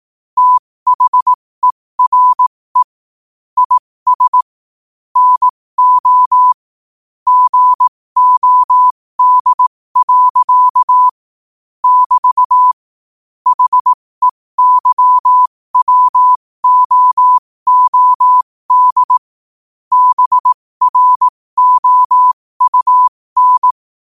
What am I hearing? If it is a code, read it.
Morse code, heard as a keyed tone, about 18 wpm, spaced as for 15 wpm, 1000 Hz. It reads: THERE IS NO GOD. <BT> HEYWOOD BROUN
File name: qotdfiles / 15wpm